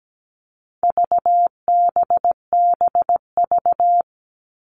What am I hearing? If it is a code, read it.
Morse code, heard as a keyed tone, about 17 wpm, 700 Hz. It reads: VBBV
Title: All Cut Numbers - 4-Digit 17wpm